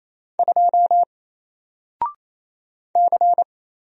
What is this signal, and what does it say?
Morse code, heard as a keyed tone, about 27 wpm, 700 Hz. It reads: 2 C
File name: Single-Letter-Number-28wpm